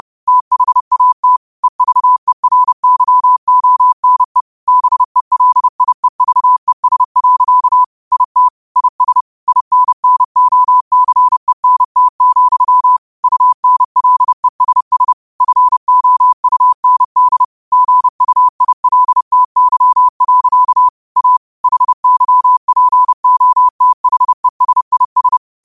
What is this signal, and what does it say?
Morse code, heard as a keyed tone, about 30 wpm, 1000 Hz. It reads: THAT EVERYONE BELIEVES. IT IS INNOCENT, UNLESS FOUND GUILTY. A HYPOTHESIS
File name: qotdfiles / 30wpm